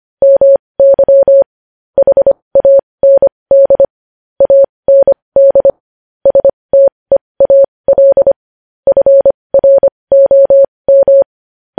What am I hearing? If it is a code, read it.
Morse code, heard as a keyed tone, about 25 wpm, spaced as for 20 wpm, 550 Hz. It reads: MY HAND AND STEAL FROM
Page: news-headlines-in-morse-code-at-20-wpm